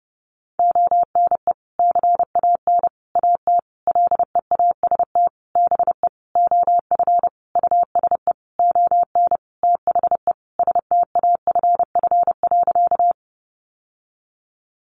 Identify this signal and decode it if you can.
Morse code, heard as a keyed tone, about 30 wpm, 700 Hz. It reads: ONE CAN AT LEAST BE OF USE ON THE STAFF.
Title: Sentences from Top 1200 Words 30wpm